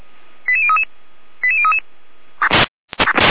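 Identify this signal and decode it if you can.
Five-tone selective call (ZVEI), heard as quick runs of five tones, tones of about 70 ms. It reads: 88033 88033
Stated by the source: ZVEI five-tone selective call received on an SDR.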